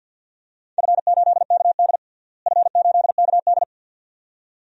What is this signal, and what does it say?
Morse code, heard as a keyed tone, about 46 wpm, 700 Hz. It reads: F8KD F8KD
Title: DXCC-100 Call Signs - 4 Characters - Repeated 2x - 50wpm